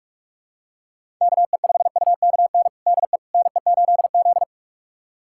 Morse code, heard as a keyed tone, about 46 wpm, 700 Hz. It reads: KE5UKN DE NE8Z
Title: USA Callsign DE Callsign 45wpm